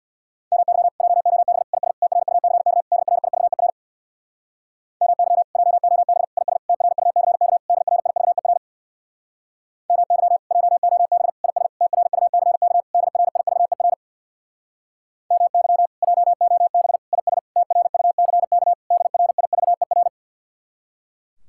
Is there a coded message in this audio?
Yes: Morse code from 0.5 to 20.1 s, heard as a keyed tone, about 67 wpm, 700 Hz. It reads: MY JOB IS TRUCK DRIVER MY JOB IS TRUCK DRIVER MY JOB IS TRUCK DRIVER MY JOB IS TRUCK DRIVER